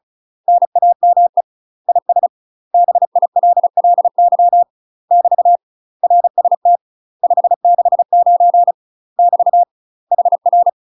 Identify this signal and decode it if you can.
Morse code, heard as a keyed tone, about 35 wpm, 700 Hz. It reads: NAME IS BILLY <BT> RST 569 <BT> HR
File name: UK MASTER QSOs TRAINING/35WPM